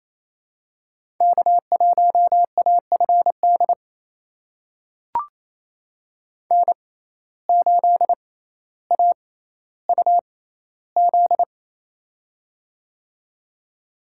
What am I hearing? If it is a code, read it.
Morse code, heard as a keyed tone, about 28 wpm, spaced as for 14 wpm, 700 Hz. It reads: K1AFD N8AUZ
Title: USA Call Signs - 5 Characters - 6x Character Spacing - 28wpm